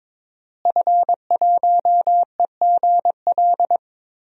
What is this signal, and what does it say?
Morse code, heard as a keyed tone, about 22 wpm, 700 Hz. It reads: F1EGL
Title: DXCC-100 Call Signs - 5 Characters - Repeated 2x - 22wpm